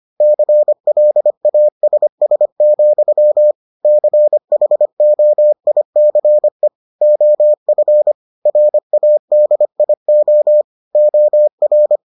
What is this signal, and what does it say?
Morse code, heard as a keyed tone, about 25 wpm, 600 Hz. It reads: CLASS, CHOICE OF RADIO OR